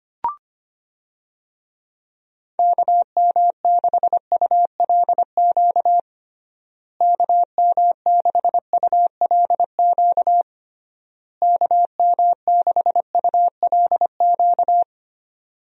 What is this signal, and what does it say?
Morse code, heard as a keyed tone, about 25 wpm, 700 Hz. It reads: KM6ULQ KM6ULQ KM6ULQ
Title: USA Call Signs - 6 Characters - Repeated 3x - 25wpm